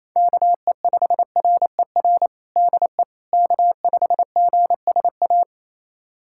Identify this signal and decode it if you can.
Morse code, heard as a keyed tone, about 28 wpm, 700 Hz. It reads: KE5RER DE K5GSA
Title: USA Callsign DE Callsign 28wpm